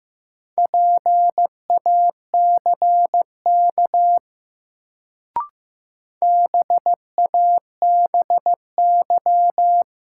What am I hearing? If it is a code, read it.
Morse code, heard as a keyed tone, about 15 wpm, 700 Hz. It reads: PACK BABY
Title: Sentences from Top 1300 Words - New Word Drill - 15wpm